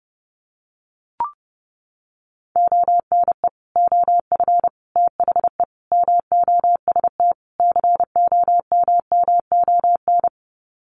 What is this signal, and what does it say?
Morse code, heard as a keyed tone, about 28 wpm, 700 Hz. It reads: ONE OF THE MOST COMMON
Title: Sets of 5 Words Encore2 30wpm